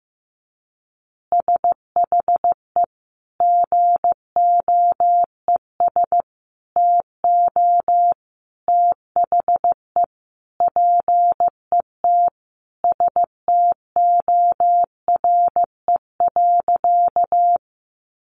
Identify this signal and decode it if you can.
Morse code, heard as a keyed tone, about 15 wpm, 700 Hz. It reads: SHE GOES TO THE PET STORE.
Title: Easy Phrases 15wpm